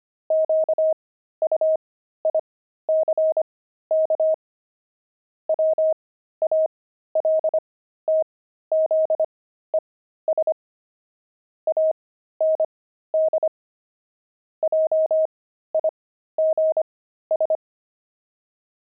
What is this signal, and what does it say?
Morse code, heard as a keyed tone, about 25 wpm, spaced as for 13 wpm, 630 Hz. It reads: QUICK WALTZES AND JIGS
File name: pangram-fs-13-cs-25-heavy